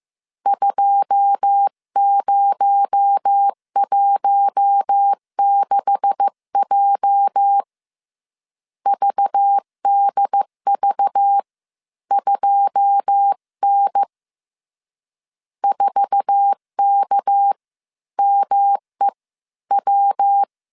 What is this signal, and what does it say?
Morse code, heard as a keyed tone, about 15 wpm, spaced as for 12 wpm, 800 Hz. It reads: 2016J VDV2N 4KMEW
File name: CW / DET / 1 9 1500C S A 14wpm 3222